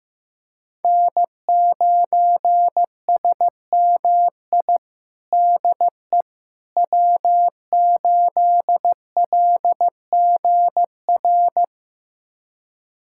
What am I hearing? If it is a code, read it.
Morse code, heard as a keyed tone, about 15 wpm, 700 Hz. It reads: N9SMI DE W8LGR
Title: USA Callsign DE Callsign - 5 Characters - 15wpm